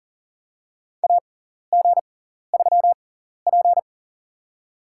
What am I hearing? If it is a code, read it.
Morse code, heard as a keyed tone, about 40 wpm, spaced as for 14 wpm, 700 Hz. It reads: AG3P
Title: USA Call Signs - 4 Characters - 6x Character Spacing - 40wpm